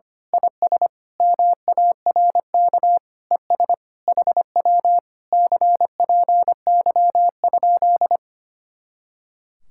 Morse code, heard as a keyed tone, about 25 wpm, 700 Hz. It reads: IS MARK ES HW CPY?
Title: Speed-Racing Common QSO Phrases Encore - 25wpm